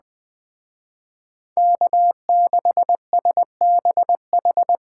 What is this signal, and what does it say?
Morse code, heard as a keyed tone, about 20 wpm, 700 Hz. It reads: K6SBH